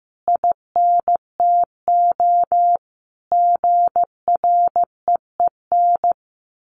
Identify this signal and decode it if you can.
Morse code, heard as a keyed tone, about 15 wpm, 700 Hz. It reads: INTO GREEN
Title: A Sound Of Thunder 15wpm